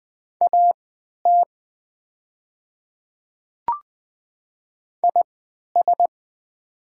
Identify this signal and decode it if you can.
Morse code, heard as a keyed tone, about 20 wpm, spaced as for 11 wpm, 700 Hz. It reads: AT IS
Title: 3x Character Spacing - Top 100 Words - 20wpm